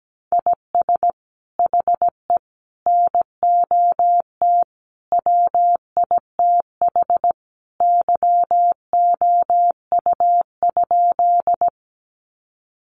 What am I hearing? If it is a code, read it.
Morse code, heard as a keyed tone, about 17 wpm, 700 Hz. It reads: IS HE NOT WITH YOU?
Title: Sentences from Top 600 Words 17wpm